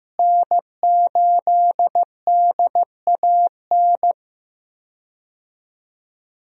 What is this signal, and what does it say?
Morse code, heard as a keyed tone, about 15 wpm, 700 Hz. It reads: N8DAN
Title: USA Call Signs - 5 Characters - Repeated 2x - 15wpm